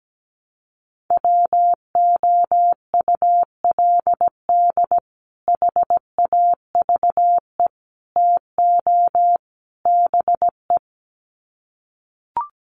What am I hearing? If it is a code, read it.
Morse code, heard as a keyed tone, about 17 wpm, 700 Hz. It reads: WOULD HAVE TO BE
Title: Sets of 4 Words 17wpm